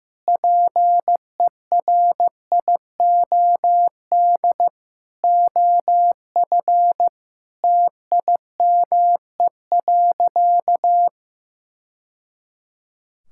Morse code, heard as a keyed tone, about 15 wpm, 700 Hz. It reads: PERIOD OF TIME.
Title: Sentences from Top 1000 Words - Review Part 5 - 15wpm